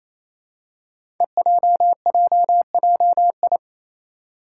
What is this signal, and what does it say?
Morse code, heard as a keyed tone, about 27 wpm, 700 Hz. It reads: EJJJI
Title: Five Challenging Letter-Number Combinations 28wpm